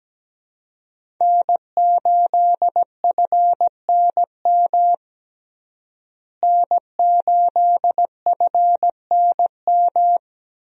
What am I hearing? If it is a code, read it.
Morse code, heard as a keyed tone, about 17 wpm, 700 Hz. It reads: N8FNM N8FNM
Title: USA Call Signs - 5 Characters - Repeated 3x - 17wpm